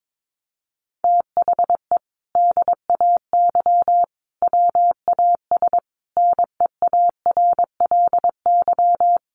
Morse code, heard as a keyed tone, about 22 wpm, 700 Hz. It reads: THE DAY WAS NEARLY
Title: Sentences from Top 1300 Words 22wpm